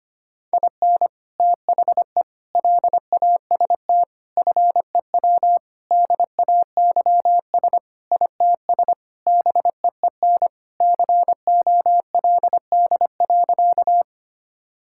Morse code, heard as a keyed tone, about 25 wpm, 700 Hz. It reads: IN THE LAST FEW DAYS ITS BEEN COLD.
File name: practice / 25wpm / Sentences with 3-Word Phrases - 25wpm